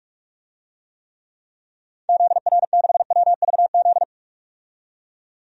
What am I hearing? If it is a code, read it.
Morse code, heard as a keyed tone, about 44 wpm, 690 Hz. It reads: ZR6WVZ
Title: DXCC-100 Call Signs - 6 Characters - 45wpm